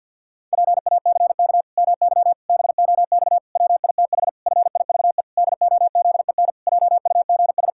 Morse code, heard as a keyed tone, about 51 wpm, 700 Hz. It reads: PACK MY BOX WITH FIVE DOZEN JUGS
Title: Pangram Challenge v2 - 50wpm